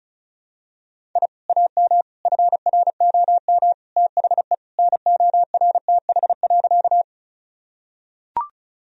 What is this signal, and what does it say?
Morse code, heard as a keyed tone, about 36 wpm, 700 Hz. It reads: I AM FROM THE NORTH.
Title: Sentences from Top 1000 Words - Review Part 1 - 35wpm